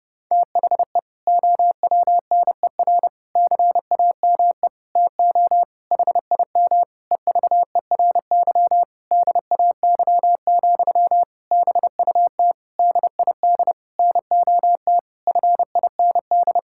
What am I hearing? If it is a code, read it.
Morse code, heard as a keyed tone, about 30 wpm, 700 Hz. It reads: THE OWNER CAME TO HIM EVERY DAY, BUT DID NOT FIND
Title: Sentences from Top 1600 Words 30wpm